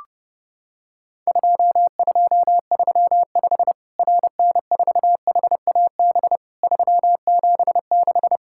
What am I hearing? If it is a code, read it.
Morse code, heard as a keyed tone, about 32 wpm, 700 Hz. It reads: 2235 RN4HAB 376
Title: Worked All Europe DX Contest - QTC Exchange - 30wpm